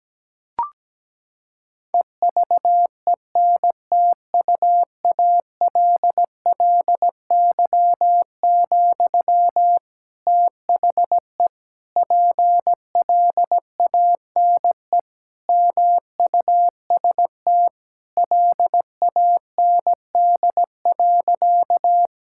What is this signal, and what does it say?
Morse code, heard as a keyed tone, about 17 wpm, 700 Hz. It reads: EVENTUALLY, THE PLANE MUST LAND.
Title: Sentences from Top 1000 Words - Review Part 6 - 17wpm